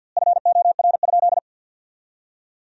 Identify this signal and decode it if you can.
Morse code, heard as a keyed tone, about 50 wpm, 700 Hz. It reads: FOR?
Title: 2x Word-Spacing - Sentences from Top 200 Words - 50wpm